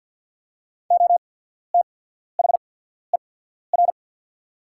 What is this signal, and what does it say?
Morse code, heard as a keyed tone, about 48 wpm, spaced as for 14 wpm, 700 Hz. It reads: OTHER
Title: 8x Character Spacing - Top 300 Words - 50wpm